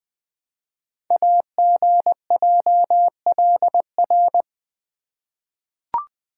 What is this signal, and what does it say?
Morse code, heard as a keyed tone, about 20 wpm, 700 Hz. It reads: AGJLR
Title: Code Groups - 5 Letters - 20wpm